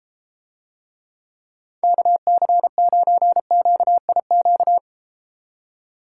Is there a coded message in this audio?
Yes: Morse code from 1.8 to 4.8 s, heard as a keyed tone, about 32 wpm, 700 Hz. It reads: KC9QIQ